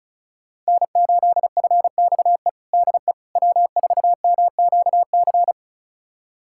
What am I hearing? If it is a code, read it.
Morse code, heard as a keyed tone, about 34 wpm, 700 Hz. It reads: N8FXE DE W4MQC
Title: USA Callsign DE Callsign 35wpm